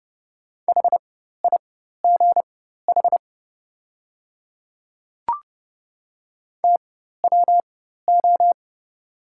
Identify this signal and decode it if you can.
Morse code, heard as a keyed tone, about 30 wpm, spaced as for 14 wpm, 700 Hz. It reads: HIGH TWO
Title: 4x Character Spacing - Top 200 Words - 30wpm